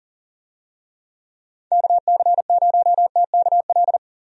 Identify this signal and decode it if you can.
Morse code, heard as a keyed tone, about 40 wpm, 700 Hz. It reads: KC0TKL
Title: USA Call Signs - 6 Characters - 40wpm